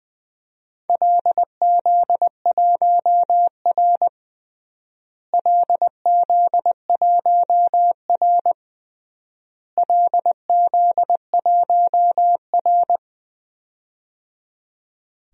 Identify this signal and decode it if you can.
Morse code, heard as a keyed tone, about 20 wpm, 700 Hz. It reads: LZ1R LZ1R LZ1R
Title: DXCC-100 Call Signs - 4 Characters - Repeated 3x - 20wpm